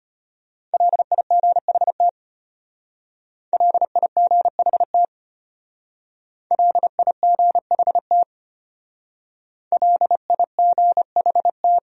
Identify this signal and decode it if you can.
Morse code, heard as a keyed tone, about 31 wpm, 700 Hz. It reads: LIGHT LIGHT LIGHT LIGHT
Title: Speed-Racing Top 200 Words 25wpm